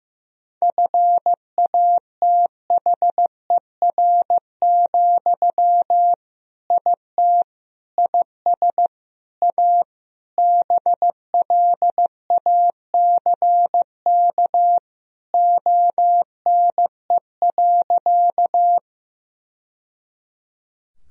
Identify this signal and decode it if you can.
Morse code, heard as a keyed tone, about 15 wpm, 700 Hz. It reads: FATHER, IT IS A BLACK ONE.
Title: Sentences from Top 1000 Words - Review Part 2 - 15wpm